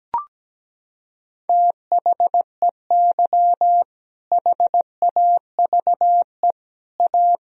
Morse code, heard as a keyed tone, about 17 wpm, 700 Hz. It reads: THEY HAVE A
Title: Sentences from Top 1000 Words - Review Part 4 - 17wpm